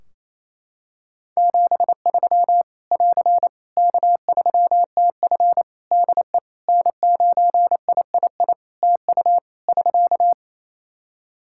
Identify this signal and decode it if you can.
Morse code, heard as a keyed tone, about 28 wpm, 700 Hz. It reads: 73 <AR> K3TF DE N9III TU <SK>
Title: Practice QSO - Exchange 4 - Ending - 28wpm